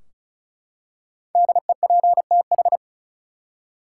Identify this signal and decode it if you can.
Morse code, heard as a keyed tone, about 36 wpm, 700 Hz. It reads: DEPTH
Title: Five Letter Words - 35wpm